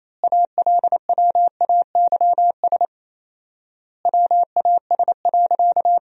Morse code, heard as a keyed tone, about 27 wpm, 700 Hz. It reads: ALWAYS WAS.
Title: 4x Word-Spacing - Sentences from Top 300 Words - 28wpm